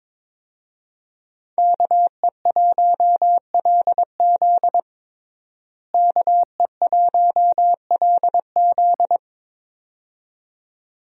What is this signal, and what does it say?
Morse code, heard as a keyed tone, about 22 wpm, 700 Hz. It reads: KE1LZ KE1LZ
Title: USA Call Signs - 5 Characters - Repeated 2x - 22wpm